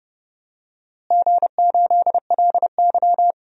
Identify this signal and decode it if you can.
Morse code, heard as a keyed tone, about 32 wpm, 700 Hz. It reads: G8LY
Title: DXCC-100 Call Signs - 4 Characters - Repeated 3x - 30wpm